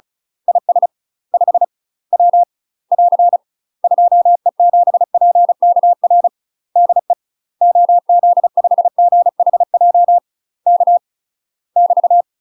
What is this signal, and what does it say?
Morse code, heard as a keyed tone, about 33 wpm, 700 Hz. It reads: IS 5 W <AR> 2E7PKR DE OZ5GHJ K <BT>
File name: UK MASTER QSOs TRAINING/35WPM